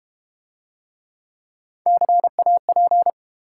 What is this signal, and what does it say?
Morse code, heard as a keyed tone, about 31 wpm, 700 Hz. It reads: CAP
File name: Easy Words - Speed-Racing with Hints - 25wpm